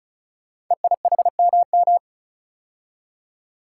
Morse code, heard as a keyed tone, about 36 wpm, 700 Hz. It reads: EIHMM
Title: Code Groups - 5 Letters - 35wpm